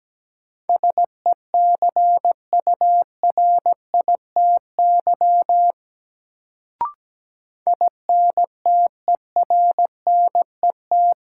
Morse code, heard as a keyed tone, about 17 wpm, 700 Hz. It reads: SECURITY INTERNET
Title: Sentences from Top 1000 Words - Review Part 4 - New Word Drill - 17wpm